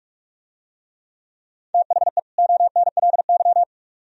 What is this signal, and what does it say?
Morse code, heard as a keyed tone, about 44 wpm, 700 Hz. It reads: THE ONLY